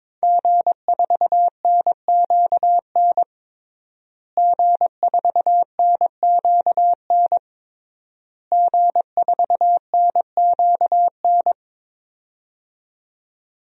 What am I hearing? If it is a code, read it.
Morse code, heard as a keyed tone, about 22 wpm, 700 Hz. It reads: G4NQN G4NQN G4NQN
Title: DXCC-100 Call Signs - 5 Characters - Repeated 3x - 22wpm